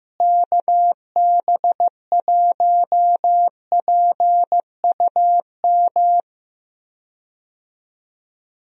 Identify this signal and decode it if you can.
Morse code, heard as a keyed tone, about 15 wpm, 700 Hz. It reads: KB1PUM
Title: USA Call Signs - 6 Characters - Repeated 2x - 15wpm